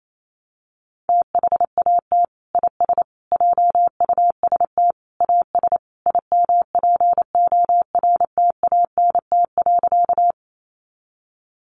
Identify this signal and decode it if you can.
Morse code, heard as a keyed tone, about 28 wpm, 700 Hz. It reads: THAT IS JUST AS IMPORTANT.
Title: Sentences from Top 600 Words 28wpm